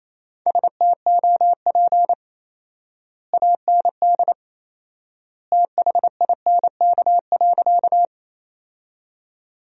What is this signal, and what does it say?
Morse code, heard as a keyed tone, about 28 wpm, 700 Hz. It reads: STOP AND THINK.
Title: 4x Word-Spacing - Sentences from Top 300 Words - 28wpm